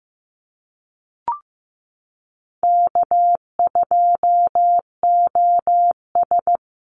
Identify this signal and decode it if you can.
Morse code, heard as a keyed tone, about 15 wpm, 700 Hz. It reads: K2OS
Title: Easy Call Signs - 15wpm